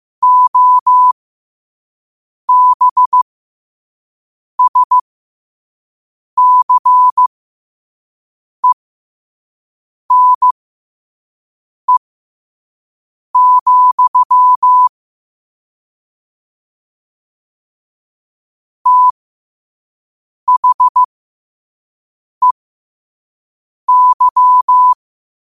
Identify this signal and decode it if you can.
Morse code, heard as a keyed tone, about 15 wpm, spaced as for 5 wpm, 1000 Hz. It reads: OBSCENE, THEY
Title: 5 WPM morse code quotes for Fri, 01 Mar 2024